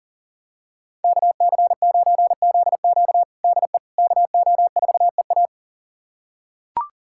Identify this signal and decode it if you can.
Morse code, heard as a keyed tone, about 40 wpm, 700 Hz. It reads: KC9ZQ DE KO4EA
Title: USA Callsign DE Callsign - 5 Characters - 40wpm